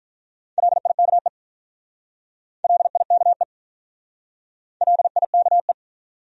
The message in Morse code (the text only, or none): LIKE LIKE LIKE